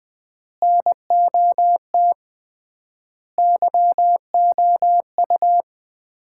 Morse code, heard as a keyed tone, about 20 wpm, 700 Hz. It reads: NOT YOU
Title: 3x Word-Spacing - Sets of 5 Words - 20wpm